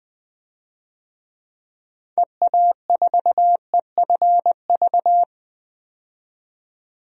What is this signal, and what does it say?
Morse code, heard as a keyed tone, about 20 wpm, 700 Hz. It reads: EA4EFV
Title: DXCC-100 Call Signs - 6 Characters - 20wpm